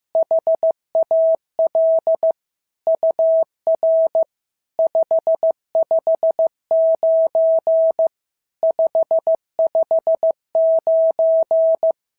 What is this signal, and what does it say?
Morse code, heard as a keyed tone, about 15 wpm, 650 Hz. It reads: HAL UR 559 559
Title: SOTA QSO Practice 15wpm